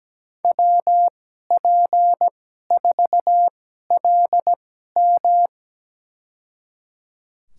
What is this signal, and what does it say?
Morse code, heard as a keyed tone, about 17 wpm, spaced as for 12 wpm, 700 Hz. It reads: WP4LM
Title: USA Call Signs - 5 Characters - 2x Character Spacing - 17wpm